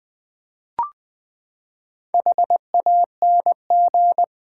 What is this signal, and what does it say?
Morse code, heard as a keyed tone, about 20 wpm, 700 Hz. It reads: HANG
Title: Four Letter Words - 20wpm